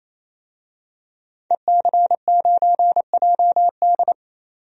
Morse code, heard as a keyed tone, about 28 wpm, 700 Hz. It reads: EC9JD
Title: DXCC-100 Call Signs - 5 Characters - 28wpm